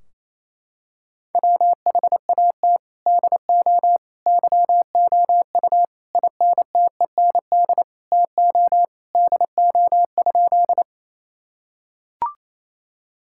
Morse code, heard as a keyed tone, about 28 wpm, 700 Hz. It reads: WHAT DO YOU INTEND TO DO?
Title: Sentences from Top 1600 Words 28wpm